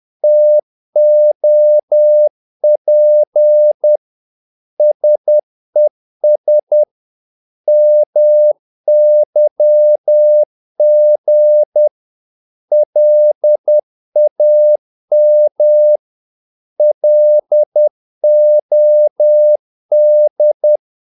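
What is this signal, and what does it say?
Morse code, heard as a keyed tone, about 10 wpm, 600 Hz. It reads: TOP SES MYG LAM LOD
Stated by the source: Morse practice of short Danish words at 10 wpm, from CW med Gnister.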